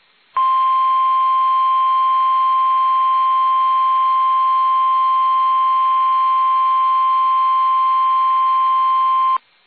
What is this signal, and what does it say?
Emergency Alert System (SAME) tones (the 1050 Hz attention tone). It is heard from 0.3 to 9.4 s.
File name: eas / KUSO / KUSO